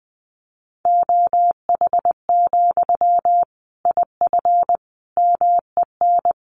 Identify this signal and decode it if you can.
Morse code, heard as a keyed tone, about 20 wpm, 700 Hz. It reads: OH, IF MEN
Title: Sentences from Top 500 Words 20wpm